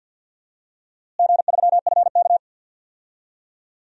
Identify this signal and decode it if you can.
Morse code, heard as a keyed tone, about 51 wpm, 700 Hz. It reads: G3FK
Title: DXCC-100 Call Signs - 4 Characters - Repeated 2x - 50wpm